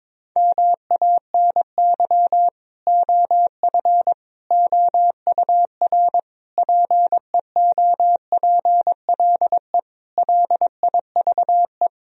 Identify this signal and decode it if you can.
Morse code, heard as a keyed tone, about 22 wpm, 690 Hz. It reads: MANY OF OUR PEOPLE LIVE
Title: Sentences from Top 1000 Words - Review Part 3 - 22wpm